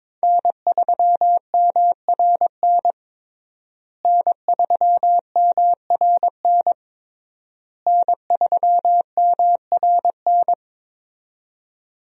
Morse code, heard as a keyed tone, about 22 wpm, 690 Hz. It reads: N3MRN N3MRN N3MRN
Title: USA Call Signs - 5 Characters - Repeated 3x - 22wpm